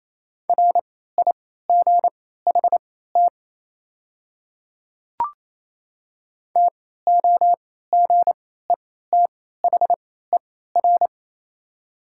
Morse code, heard as a keyed tone, about 28 wpm, spaced as for 16 wpm, 700 Hz. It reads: RIGHT TOGETHER
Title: 3x Character Spacing - Top 300 Words - 28wpm